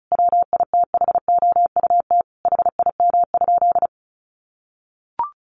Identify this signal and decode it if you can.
Morse code, heard as a keyed tone, about 34 wpm, 700 Hz. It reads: WITHOUT HIM?
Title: Sentences from Top 1500 Words 35wpm